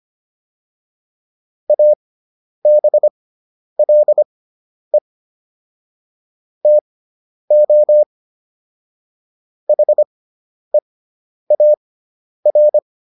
Morse code, heard as a keyed tone, about 25 wpm, spaced as for 10 wpm, 600 Hz. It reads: ABLE TO HEAR